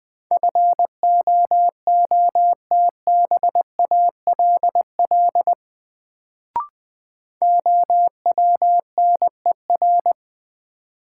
Morse code, heard as a keyed tone, about 21 wpm, 700 Hz. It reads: FOOTBALL OWNER
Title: Sentences from Top 1600 Words - New Word Drill - 20wpm